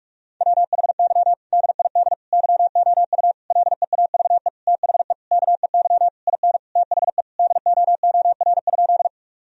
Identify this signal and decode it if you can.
Morse code, heard as a keyed tone, about 46 wpm, 700 Hz. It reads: WHY DID YOU LEAVE THE KEY IN THE DOOR?
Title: Sentences from Top 1000 Words - Review Part 4 - 45wpm